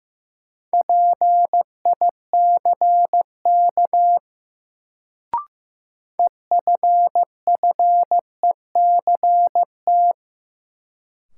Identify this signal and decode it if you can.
Morse code, heard as a keyed tone, about 15 wpm, 700 Hz. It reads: PICK EFFECT
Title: Sentences from Top 1000 Words - New Word Drill - 15wpm